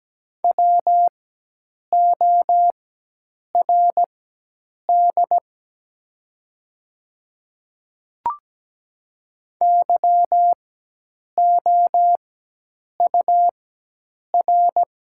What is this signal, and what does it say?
Morse code, heard as a keyed tone, about 17 wpm, spaced as for 8 wpm, 700 Hz. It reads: WORD YOUR
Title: 4x Character Spacing - Top 100 Words - 17wpm